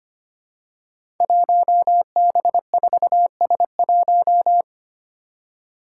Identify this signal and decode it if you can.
Morse code, heard as a keyed tone, about 25 wpm, 700 Hz. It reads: 1B4S1
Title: Five Challenging Letter-Number Combinations 25wpm